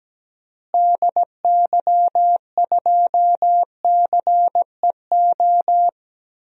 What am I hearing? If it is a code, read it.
Morse code, heard as a keyed tone, about 17 wpm, 700 Hz. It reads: DY2CEO
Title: DXCC-100 Call Signs - 6 Characters - Repeated 3x - 17wpm